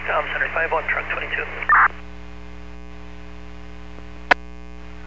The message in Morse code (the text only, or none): none